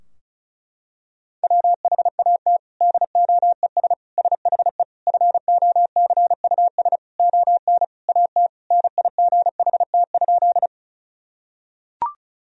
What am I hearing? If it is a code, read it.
Morse code, heard as a keyed tone, about 36 wpm, 700 Hz. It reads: WHAT DOES SHE FOCUS ON AT NIGHT?
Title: Sentences from Top 1000 Words - Review Part 4 - 35wpm